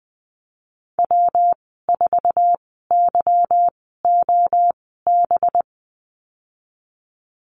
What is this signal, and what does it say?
Morse code, heard as a keyed tone, about 20 wpm, spaced as for 14 wpm, 700 Hz. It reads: W4YOB
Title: 186 Lesson 30g - Letter b - Call Signs 20wpm